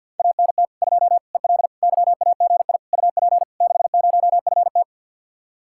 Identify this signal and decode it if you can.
Morse code, heard as a keyed tone, about 67 wpm, spaced as for 58 wpm, 700 Hz. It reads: ANT 2 EL YAGI UP 60FT
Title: Speed-Racing Common QSO Phrases Encore - 50wpm